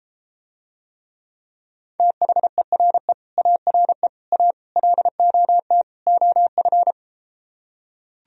Morse code, heard as a keyed tone, about 33 wpm, 700 Hz. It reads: THERE ARE A LOT OF